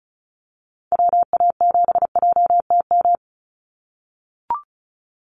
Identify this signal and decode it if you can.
Morse code, heard as a keyed tone, about 34 wpm, 700 Hz. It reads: WA7JTM
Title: NA SOTA Activator Callsigns 35wpm